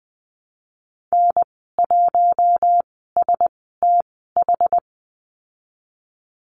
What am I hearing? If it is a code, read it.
Morse code, heard as a keyed tone, about 20 wpm, spaced as for 14 wpm, 700 Hz. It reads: N1STH
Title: 147 Lesson 24g - Letter y - Call Signs 20wpm